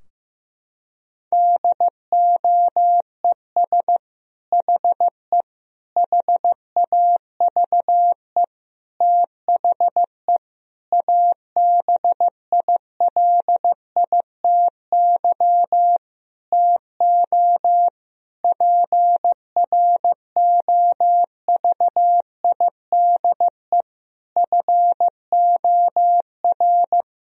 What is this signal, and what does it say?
Morse code, heard as a keyed tone, about 15 wpm, 700 Hz. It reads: DOES HE HAVE THE ABILITY TO PROVIDE FOR
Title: Sentences from Top 1000 Words - Review Part 3 - 15wpm